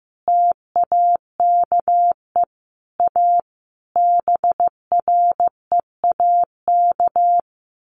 Morse code, heard as a keyed tone, about 15 wpm, 700 Hz. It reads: TAKE A BREAK
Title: Sets of 5 Words Encore 15wpm